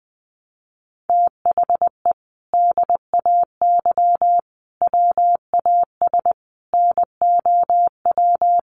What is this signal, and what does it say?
Morse code, heard as a keyed tone, about 20 wpm, 700 Hz. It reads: THE DAY WAS NOW
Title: Sentences from Top 1200 Words 20wpm